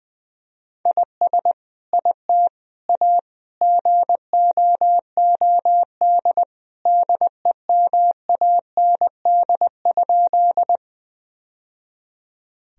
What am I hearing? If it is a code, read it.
Morse code, heard as a keyed tone, about 21 wpm, 700 Hz. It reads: IS IT A GOOD DEMAND?
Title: Sentences from Top 1000 Words - Review Part 4 - 20wpm